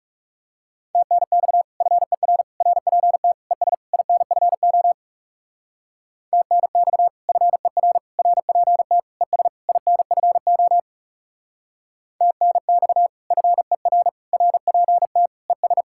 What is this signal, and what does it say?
Morse code, heard as a keyed tone, about 39 wpm, 700 Hz. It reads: TNX FER RPT ES INFO TNX FER RPT ES INFO TNX FER RPT ES